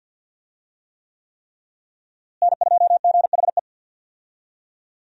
Morse code, heard as a keyed tone, about 53 wpm, 700 Hz. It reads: N2GHE
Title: USA Call Signs - 5 Characters - Repeated 3x - 50wpm